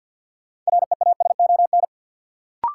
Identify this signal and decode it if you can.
Morse code, heard as a keyed tone, about 51 wpm, 700 Hz. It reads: REASON
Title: Sets of 5 Words Encore - New Word Drill - 50wpm